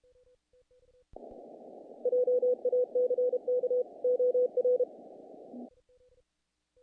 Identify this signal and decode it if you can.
Morse code, heard as a keyed tone, about 31 wpm, 500 Hz. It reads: JACK OR
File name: contests / naqp-cw-k5tr-op-k5pi- -1800